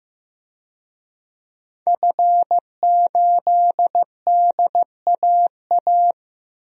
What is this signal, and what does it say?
Morse code, heard as a keyed tone, about 15 wpm, 700 Hz. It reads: F8DAA
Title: DXCC-100 Call Signs - 5 Characters - Repeated 3x - 15wpm